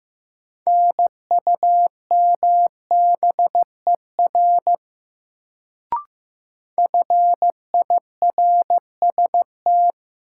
Morse code, heard as a keyed tone, about 15 wpm, 700 Hz. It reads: NUMBER FIRST
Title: Sentences from Top 100 Words - New Word Drill - 15wpm